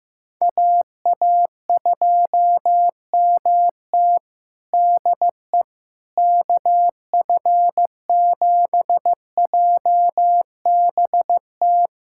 Morse code, heard as a keyed tone, about 15 wpm, 700 Hz. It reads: AA2MT DE KF7JBT
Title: USA Callsign DE Callsign 15wpm